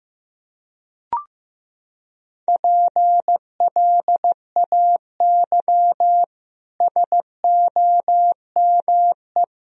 Morse code, heard as a keyed tone, about 15 wpm, 700 Hz. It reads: PLAY SOME
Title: Sentences from Top 1000 Words - Review Part 3 - 15wpm